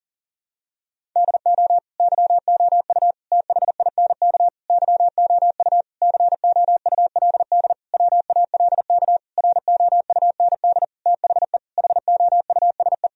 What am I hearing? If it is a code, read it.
Morse code, heard as a keyed tone, about 40 wpm, 700 Hz. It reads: DO YOU THINK YOU COULD WALK ROUND THE HOUSE
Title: Sentences from Top 1000 Words - Review Part 1 - 40wpm